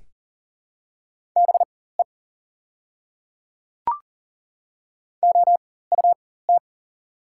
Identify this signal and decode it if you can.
Morse code, heard as a keyed tone, about 40 wpm, spaced as for 19 wpm, 700 Hz. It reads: BE OUT
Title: 4x Character Spacing - Top 100 Words - 40wpm